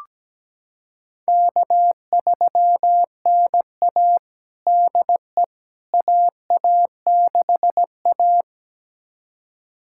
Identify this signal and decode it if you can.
Morse code, heard as a keyed tone, about 17 wpm, 700 Hz. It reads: K3NA DE AA6A
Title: USA Callsign DE Callsign - 4 Characters - 17wpm